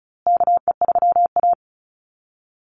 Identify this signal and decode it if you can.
Morse code, heard as a keyed tone, about 34 wpm, 700 Hz. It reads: KE3A
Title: USA Call Signs Repeated 2x 35wpm